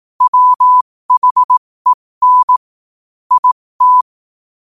Morse code, heard as a keyed tone, about 18 wpm, spaced as for 15 wpm, 1000 Hz. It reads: WHEN IT
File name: qotdfiles / 15wpm